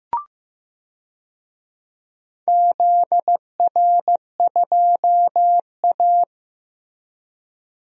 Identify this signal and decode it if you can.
Morse code, heard as a keyed tone, about 15 wpm, 700 Hz. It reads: ZR2A
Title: DXCC-100 Call Signs - 4 Characters - Repeated 2x - 15wpm